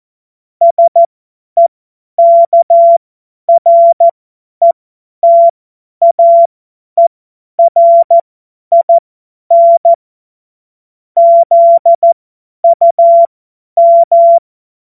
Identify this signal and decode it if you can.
Morse code, heard as a keyed tone, about 14 wpm, spaced as for 10 wpm, 680 Hz. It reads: SEKRETAERIN ZUM
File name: Morsekurs / Witze / Witz 27. Dezember 2022 50 F70